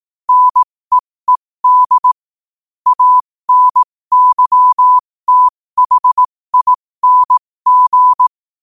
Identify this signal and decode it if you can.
Morse code, heard as a keyed tone, about 18 wpm, spaced as for 15 wpm, 1000 Hz. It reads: NEED ANYTHING